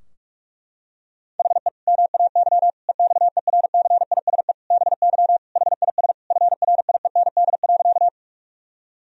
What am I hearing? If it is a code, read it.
Morse code, heard as a keyed tone, about 41 wpm, 700 Hz. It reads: HE MAY EXERCISE BY HIS FRIEND.
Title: Sentences from Top 1000 Words - Review Part 4 - 45wpm